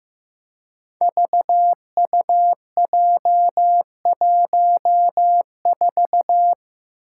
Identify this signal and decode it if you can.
Morse code, heard as a keyed tone, about 15 wpm, 700 Hz. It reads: VUJ14
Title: Five Challenging Letter-Number Combinations 15wpm